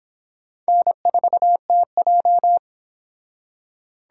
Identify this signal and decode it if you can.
Morse code, heard as a keyed tone, about 25 wpm, 700 Hz. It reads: N4TJ